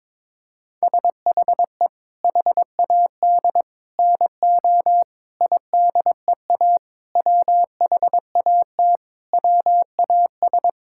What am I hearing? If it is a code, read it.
Morse code, heard as a keyed tone, about 21 wpm, 700 Hz. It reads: SHE HAD NO IDEA WHAT WAS
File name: Sentences with 7-Word Phrases - 22wpm